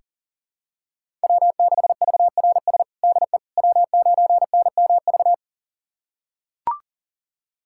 Morse code, heard as a keyed tone, about 40 wpm, 700 Hz. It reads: W6VRS DE W9NMV
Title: USA Callsign DE Callsign - 5 Characters - 40wpm